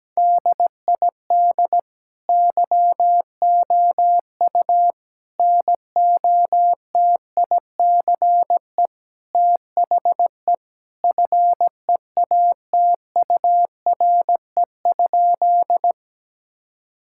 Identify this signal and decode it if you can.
Morse code, heard as a keyed tone, about 17 wpm, 700 Hz. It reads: DID YOU NOTICE THE FEATURE?
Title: Sentences from Top 1000 Words - Review Part 5 - 17wpm